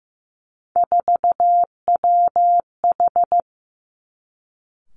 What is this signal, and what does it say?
Morse code, heard as a keyed tone, about 15 wpm, 700 Hz. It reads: 4WH
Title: Three Challenging Letter-Number Combinations 15wpm